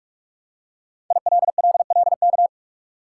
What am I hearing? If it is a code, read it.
Morse code, heard as a keyed tone, about 47 wpm, 700 Hz. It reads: ILLLK
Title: Code Groups - 5 Letters - 45wpm